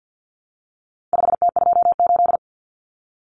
Morse code, heard as a keyed tone, about 50 wpm, 700 Hz. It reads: 5T28
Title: Common Cut Numbers - 4-Digit 50wpm